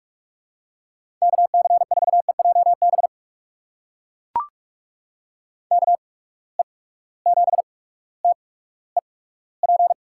Morse code, heard as a keyed tone, about 44 wpm, spaced as for 19 wpm, 700 Hz. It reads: KC4EJB KE7TEP
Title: USA Call Signs - 8x Character Spacing - 45wpm